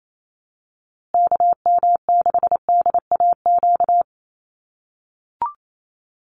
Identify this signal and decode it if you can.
Morse code, heard as a keyed tone, about 28 wpm, 700 Hz. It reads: KM6DAQ
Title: USA Call Signs Repeated 3x 28wpm